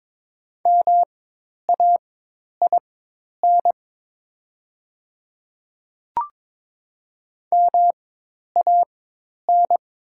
Morse code, heard as a keyed tone, about 22 wpm, spaced as for 10 wpm, 700 Hz. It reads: MAIN MAN
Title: 4x Character Spacing - Top 300 Words - 22wpm